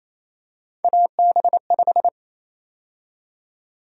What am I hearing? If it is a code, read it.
Morse code, heard as a keyed tone, about 28 wpm, 700 Hz. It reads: AB5
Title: USA Call Sign Prefixes - 28wpm